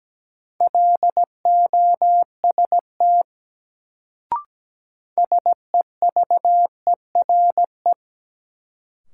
Morse code, heard as a keyed tone, about 17 wpm, 700 Hz. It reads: LOST SEVERE
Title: Sentences from Top 1400 Words - New Word Drill Alt - 17wpm